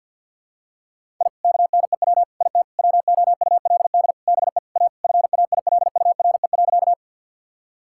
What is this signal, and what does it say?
Morse code, heard as a keyed tone, about 52 wpm, 700 Hz. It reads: I KNEW IT WOULD BE A FAILURE.
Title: Sentences from Top 1000 Words - Review Part 6 - 50wpm